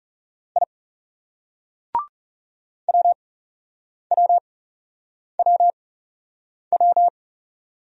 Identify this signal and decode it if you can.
Morse code, heard as a keyed tone, about 36 wpm, spaced as for 8 wpm, 700 Hz. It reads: I WWWW
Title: Speed-Racing Single Letters 30wpm